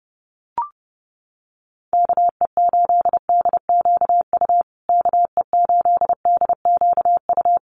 Morse code, heard as a keyed tone, about 30 wpm, 700 Hz. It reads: KE8DQU KE8DQU
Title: USA Call Signs Repeated 3x 30wpm